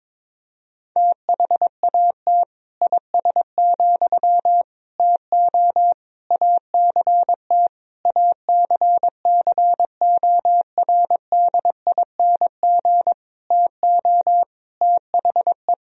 Morse code, heard as a keyed tone, about 22 wpm, 700 Hz. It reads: THAT IS, TO ACT ACCORDING TO THE
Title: Sentences from Top 1000 Words - Review Part 4 - 22wpm